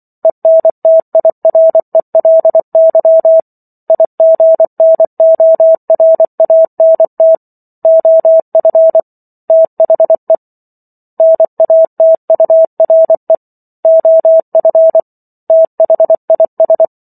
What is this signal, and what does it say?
Morse code, heard as a keyed tone, about 24 wpm, 650 Hz. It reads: ENTIRELY IGNORANT OF THE NATURE OF THIS